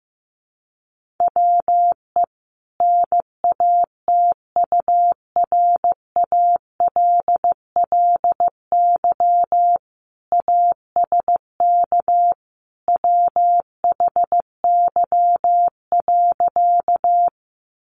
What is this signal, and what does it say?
Morse code, heard as a keyed tone, about 15 wpm, 700 Hz. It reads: WE NATURALLY ASK WHY.
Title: Sentences from Top 1300 Words 15wpm